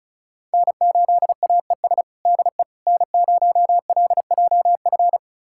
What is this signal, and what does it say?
Morse code, heard as a keyed tone, about 34 wpm, 700 Hz. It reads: N8AES DE N0LJF
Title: USA Callsign DE Callsign - 5 Characters - 35wpm